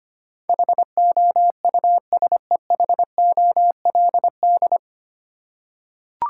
Morse code, heard as a keyed tone, about 25 wpm, 690 Hz. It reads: HOUSEHOLD
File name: practice / 25wpm / 9 Letter Words - First 3 Letters Given - 25wpm